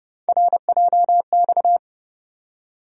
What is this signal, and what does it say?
Morse code, heard as a keyed tone, about 32 wpm, 700 Hz. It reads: RJX
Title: USA Call Sign Postfixes - 3 Letters 30wpm